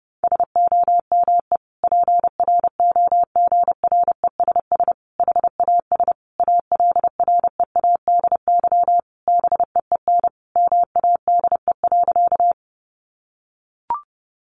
Morse code, heard as a keyed tone, about 30 wpm, 700 Hz. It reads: SOME PROGRESS HAS ALREADY BEEN MADE.
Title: Sentences from Top 1400 Words 30wpm